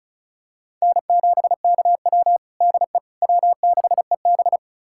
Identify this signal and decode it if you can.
Morse code, heard as a keyed tone, about 36 wpm, 700 Hz. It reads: N7KW DE W6EB
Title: USA Callsign DE Callsign - 4 Characters - 35wpm